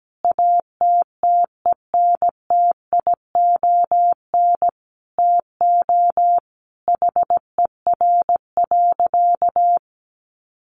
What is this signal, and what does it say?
Morse code, heard as a keyed tone, about 17 wpm, 700 Hz. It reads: ATTENTION TO HER.
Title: Sentences from Top 1000 Words 17wpm